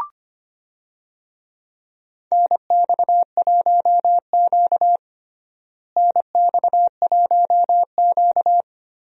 Morse code, heard as a keyed tone, about 25 wpm, 700 Hz. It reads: NX1Q NX1Q
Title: USA Call Signs - 4 Characters - Repeated 3x - 25wpm